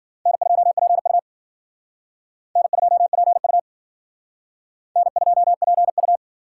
Morse code, heard as a keyed tone, about 52 wpm, 700 Hz. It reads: N2PU N2PU N2PU